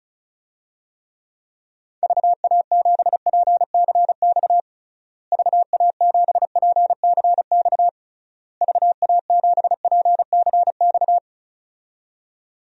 Morse code, heard as a keyed tone, about 34 wpm, 700 Hz. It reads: VA7PCX VA7PCX VA7PCX
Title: DXCC-100 Call Signs - 6 Characters - Repeated 3x - 35wpm